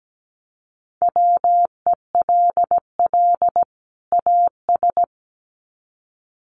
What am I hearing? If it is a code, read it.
Morse code, heard as a keyed tone, about 17 wpm, 700 Hz. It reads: WELL AS
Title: Sets of 2 Words 17wpm